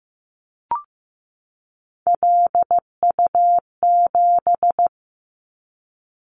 Morse code, heard as a keyed tone, about 15 wpm, 700 Hz. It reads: LU7
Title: Callsign Prefixes for Top 100 DXCC - 15wpm